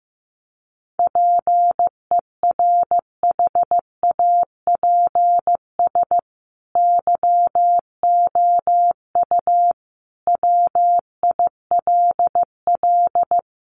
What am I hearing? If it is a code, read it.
Morse code, heard as a keyed tone, about 15 wpm, 700 Hz. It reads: PERHAPS YOU WILL
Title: Sentences from Top 1400 Words 15wpm